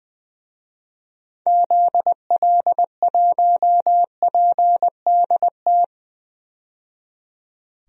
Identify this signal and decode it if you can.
Morse code, heard as a keyed tone, about 20 wpm, 700 Hz. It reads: ZL1PDT
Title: DXCC-100 Call Signs - 6 Characters - 20wpm